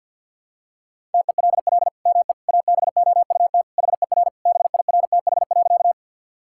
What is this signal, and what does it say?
Morse code, heard as a keyed tone, about 50 wpm, 700 Hz. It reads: TELL ME ABOUT HER BIRTH.
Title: Sentences from Top 1000 Words - Review Part 5 - 50wpm